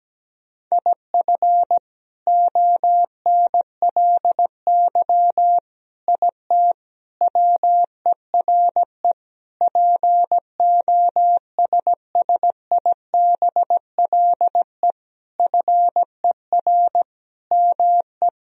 Morse code, heard as a keyed tone, about 17 wpm, 700 Hz. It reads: IF ONLY IT WERE POSSIBLE FER ME
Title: Sentences from Top 500 Words and Abbreviations 17wpm